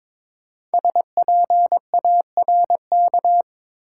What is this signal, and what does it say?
Morse code, heard as a keyed tone, about 23 wpm, 700 Hz. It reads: SPARK
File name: 5 Letter Words - First 3 Letters Given - 22wpm